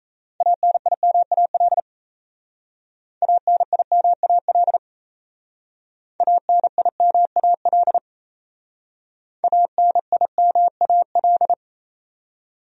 Morse code, heard as a keyed tone, about 35 wpm, 700 Hz. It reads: ANIMAL ANIMAL ANIMAL ANIMAL
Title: Speed-Racing Top 200 Words 28wpm